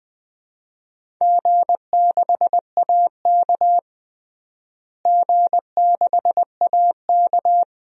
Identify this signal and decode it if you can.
Morse code, heard as a keyed tone, about 20 wpm, 700 Hz. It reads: G6AK G6AK
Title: DXCC-100 Call Signs - 4 Characters - Repeated 2x - 20wpm